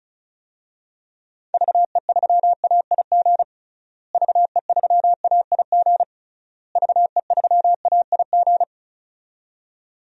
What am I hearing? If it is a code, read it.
Morse code, heard as a keyed tone, about 34 wpm, 700 Hz. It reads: VE3AIG VE3AIG VE3AIG
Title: DXCC-100 Call Signs - 6 Characters - Repeated 3x - 35wpm